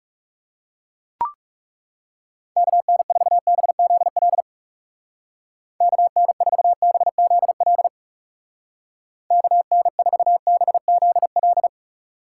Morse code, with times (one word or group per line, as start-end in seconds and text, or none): 2.6-4.4 KN4BZL
5.8-7.9 KN4BZL
9.3-11.7 KN4BZL